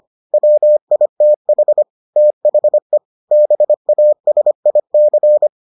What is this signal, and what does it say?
Morse code, heard as a keyed tone, about 25 wpm, 600 Hz. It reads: WITH THE BASIC